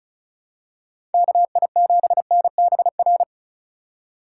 Morse code, heard as a keyed tone, about 36 wpm, 700 Hz. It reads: KI7NBR
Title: USA Call Signs - 6 Characters - 2x Character Spacing - 35wpm